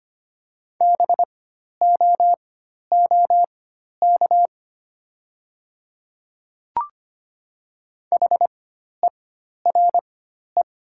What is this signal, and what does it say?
Morse code, heard as a keyed tone, about 25 wpm, spaced as for 12 wpm, 700 Hz. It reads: BOOK HERE
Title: 4x Character Spacing - Top 300 Words - 25wpm